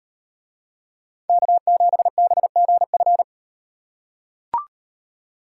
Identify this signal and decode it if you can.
Morse code, heard as a keyed tone, about 38 wpm, 700 Hz. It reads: K7BGF